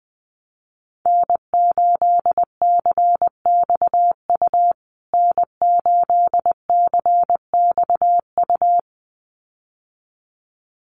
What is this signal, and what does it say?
Morse code, heard as a keyed tone, about 20 wpm, 700 Hz. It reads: N8CXU N8CXU
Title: USA Call Signs Repeated 2x 20wpm